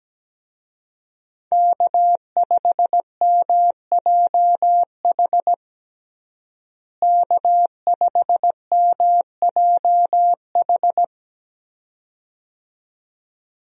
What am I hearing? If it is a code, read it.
Morse code, heard as a keyed tone, about 17 wpm, 700 Hz. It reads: K5MJH K5MJH
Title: USA Call Signs - 5 Characters - Repeated 2x - 17wpm